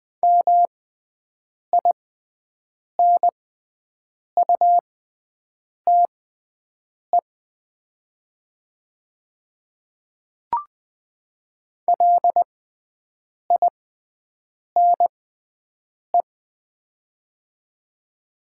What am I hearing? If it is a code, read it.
Morse code, heard as a keyed tone, about 20 wpm, spaced as for 7 wpm, 690 Hz. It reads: MINUTE LINE
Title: 6x Character Spacing - Top 500 Words - 20wpm